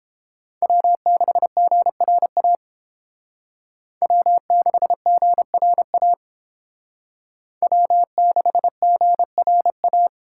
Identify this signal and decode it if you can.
Morse code, heard as a keyed tone, about 29 wpm, 700 Hz. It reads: W6GRA W6GRA W6GRA